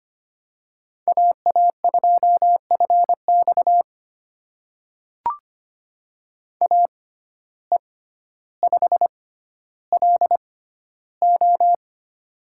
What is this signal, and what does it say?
Morse code, heard as a keyed tone, about 25 wpm, spaced as for 13 wpm, 700 Hz. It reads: AA2FX AE5LO
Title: USA Call Signs - 5 Characters - 6x Character Spacing - 25wpm